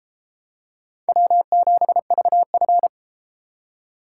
Morse code, heard as a keyed tone, about 32 wpm, 700 Hz. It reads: W7VF